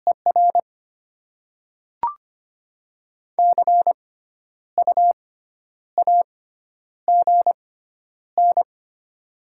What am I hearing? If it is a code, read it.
Morse code, heard as a keyed tone, about 25 wpm, spaced as for 10 wpm, 690 Hz. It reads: ER CU AGN
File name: Top 20 QSO Elements - 6x Character Spacing - 25wpm